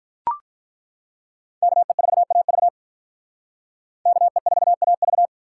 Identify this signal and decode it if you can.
Morse code, heard as a keyed tone, about 43 wpm, 700 Hz. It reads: KE4AV KE4AV
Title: Speed Racing - USA 2x2 Call Signs 35wpm